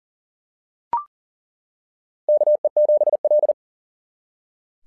Morse code, heard as a keyed tone, about 40 wpm, 600 Hz. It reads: KE7L
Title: SOTA QSO Practice 40wpm